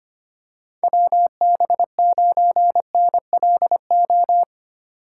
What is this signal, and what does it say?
Morse code, heard as a keyed tone, about 25 wpm, 700 Hz. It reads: WB9NLO
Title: USA Call Signs - 4x Character Spacing - 25wpm